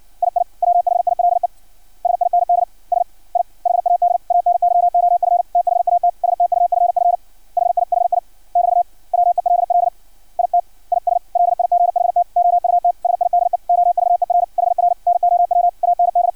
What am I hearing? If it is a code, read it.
Morse code, heard as a keyed tone, about 56 wpm, 700 Hz. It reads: II WHILE SINDS I HAD ANY OF THAT STUFF HIHI <BT> WELL IT IS LIGHT OUTSIDE OVER HR NOW AND
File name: archived / CRSnet